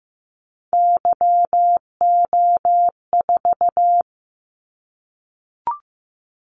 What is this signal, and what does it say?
Morse code, heard as a keyed tone, about 15 wpm, 700 Hz. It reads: YO4
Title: Callsign Prefixes for Top 25 DXCC - 15wpm